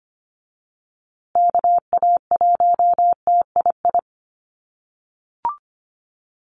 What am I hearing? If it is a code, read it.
Morse code, heard as a keyed tone, about 25 wpm, 700 Hz. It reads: KA1TII
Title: USA Call Signs Repeated 3x 25wpm